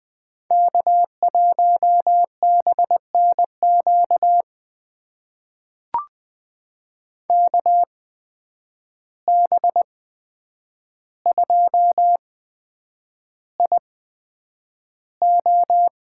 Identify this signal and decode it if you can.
Morse code, heard as a keyed tone, about 20 wpm, spaced as for 9 wpm, 700 Hz. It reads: K1BNQ KB2IO
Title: USA Call Signs - 5 Characters - 8x Character Spacing - 20wpm